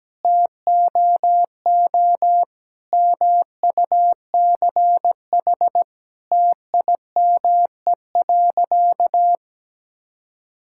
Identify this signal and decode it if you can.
Morse code, heard as a keyed tone, about 17 wpm, 700 Hz. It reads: TOO MUCH TIME.
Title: Sentences from Top 1000 Words - Review Part 6 - 17wpm